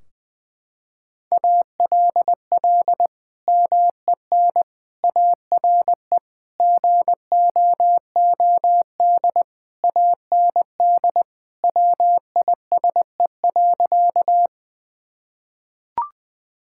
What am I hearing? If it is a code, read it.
Morse code, heard as a keyed tone, about 20 wpm, 700 Hz. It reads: ALL MEN ARE GOOD AND WISE.
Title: Sentences from Top 1600 Words 20wpm